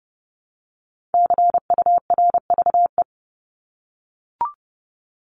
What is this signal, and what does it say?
Morse code, heard as a keyed tone, about 30 wpm, 700 Hz. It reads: CURVE
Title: Top 1501-1750 Words 30wpm